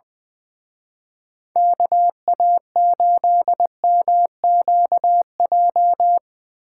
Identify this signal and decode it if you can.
Morse code, heard as a keyed tone, about 23 wpm, 700 Hz. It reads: KA8MQJ
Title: Speed Racing - USA 2x3 Call Signs 20wpm